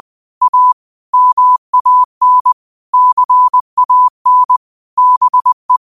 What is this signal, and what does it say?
Morse code, heard as a keyed tone, about 20 wpm, 1000 Hz. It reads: A MAN CAN BE
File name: qotdfiles / 20wpm